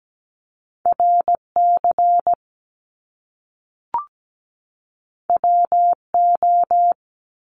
Morse code, heard as a keyed tone, about 17 wpm, 700 Hz. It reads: RC WO
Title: Top Two Letter Combinations in Words 17wpm